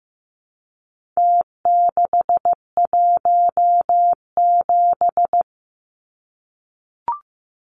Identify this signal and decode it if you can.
Morse code, heard as a keyed tone, about 15 wpm, 700 Hz. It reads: T617
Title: Numbers - Contest Style - 4-Digit 15wpm